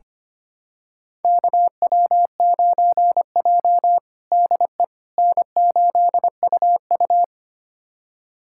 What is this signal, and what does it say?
Morse code, heard as a keyed tone, about 25 wpm, 700 Hz. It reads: KW9J DE N8UU
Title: USA Callsign DE Callsign - 4 Characters - 25wpm